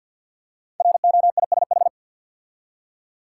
Morse code, heard as a keyed tone, about 47 wpm, 700 Hz. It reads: ROISH
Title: Code Groups - 5 Letters - 50wpm